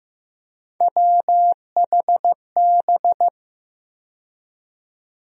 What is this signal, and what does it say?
Morse code, heard as a keyed tone, about 15 wpm, 700 Hz. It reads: WHB
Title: USA Call Sign Postfixes - 3 Letters 15wpm